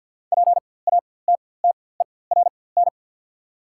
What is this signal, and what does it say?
Morse code, heard as a keyed tone, about 48 wpm, spaced as for 23 wpm, 700 Hz. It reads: PATTERN
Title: 4x Character Spacing - Top 500 Words - 50wpm